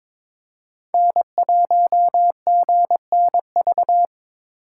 Morse code, heard as a keyed tone, about 26 wpm, 700 Hz. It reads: N1GNV
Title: Speed Racing - USA 1x3 Call Signs 22wpm